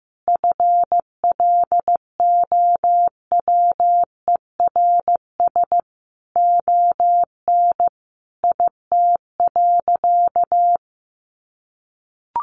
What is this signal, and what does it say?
Morse code, heard as a keyed tone, about 15 wpm, 700 Hz. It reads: FLOWERS ON IT.
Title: Additional Easy Phrases 15wpm